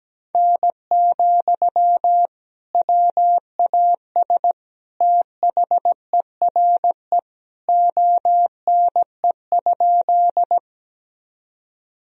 Morse code, heard as a keyed tone, about 17 wpm, 700 Hz. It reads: N, WAS THERE ONE?
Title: Sentences from Top 100 Words and Abbreviations 17wpm